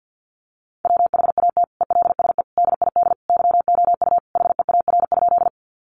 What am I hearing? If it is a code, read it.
Morse code, heard as a keyed tone, about 50 wpm, 700 Hz. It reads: WHAT ELSE DID YOU HEAR?
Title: Sentences from Top 800 Words 50wpm